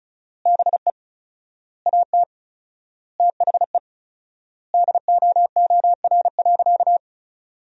Morse code, heard as a keyed tone, about 36 wpm, 700 Hz. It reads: BE AT THE DOOR.
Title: 4x Word-Spacing - Sentences from Top 400 Words - 35wpm